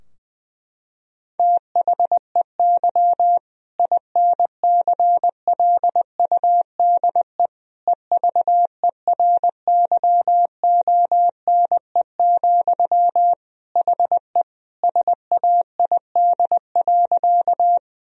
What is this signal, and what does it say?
Morse code, heard as a keyed tone, about 21 wpm, 700 Hz. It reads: THEY INCLUDE EVERYONE, HE SAID.
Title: Sentences from Top 1000 Words - Review Part 3 - 20wpm